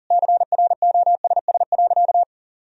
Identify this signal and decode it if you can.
Morse code, heard as a keyed tone, about 40 wpm, 700 Hz. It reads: CROSS.
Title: Sentences from Top 1000 Words - Review Part 2 - 40wpm